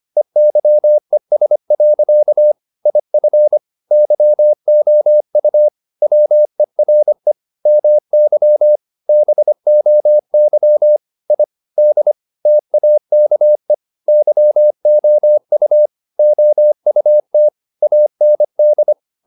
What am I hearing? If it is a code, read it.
Morse code, heard as a keyed tone, about 25 wpm, 600 Hz. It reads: EYES. IF YOU WERE MY BOY I D TAKE YOU OUT AND
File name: the old man and the sea part 1 25wpm